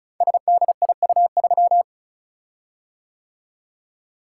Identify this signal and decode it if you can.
Morse code, heard as a keyed tone, about 36 wpm, 700 Hz. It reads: SDIU3
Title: Code Groups - 5 Letter:Numbers - 35wpm